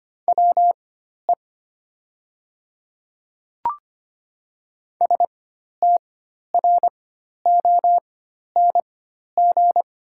Morse code, heard as a keyed tone, about 25 wpm, spaced as for 12 wpm, 700 Hz. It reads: WE STRONG
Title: 4x Character Spacing - Top 500 Words - 25wpm